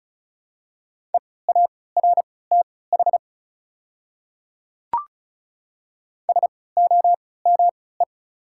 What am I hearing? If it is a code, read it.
Morse code, heard as a keyed tone, about 34 wpm, spaced as for 20 wpm, 700 Hz. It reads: EARTH SOME
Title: 3x Character Spacing - Top 200 Words - 35wpm